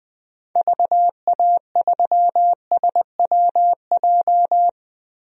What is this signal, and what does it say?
Morse code, heard as a keyed tone, about 20 wpm, 700 Hz. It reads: VA3SWJ
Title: DXCC-100 Call Signs - 6 Characters - 20wpm